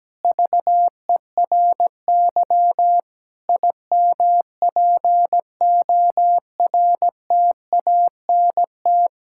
Morse code, heard as a keyed tone, about 17 wpm, 700 Hz. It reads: VERY IMPORTANT
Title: Sentences from Top 1000 Words - Review Part 5 - 17wpm